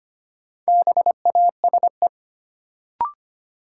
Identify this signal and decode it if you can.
Morse code, heard as a keyed tone, about 25 wpm, 700 Hz. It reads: BASE
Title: Sentences from Top 1600 Words - New Word Drill Alt - 25wpm